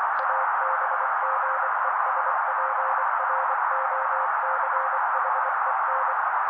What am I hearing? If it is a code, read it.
Morse code, heard as a keyed tone, about 24 wpm, 530 Hz. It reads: ABGESPROCHEN
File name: daily / 23wpm noise